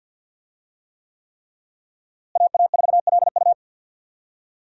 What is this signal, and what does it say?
Morse code, heard as a keyed tone, about 46 wpm, 700 Hz. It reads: AA4LU
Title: USA Call Signs - 5 Characters - Repeated 3x - 50wpm